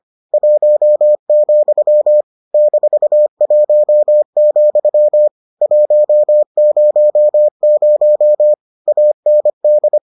Morse code, heard as a keyed tone, about 25 wpm, 600 Hz. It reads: 1, -1, 100 AND